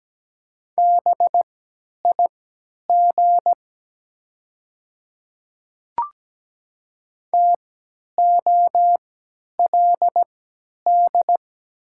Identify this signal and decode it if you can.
Morse code, heard as a keyed tone, about 17 wpm, spaced as for 10 wpm, 700 Hz. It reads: BIG TOLD
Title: 3x Character Spacing - Top 500 Words - 17wpm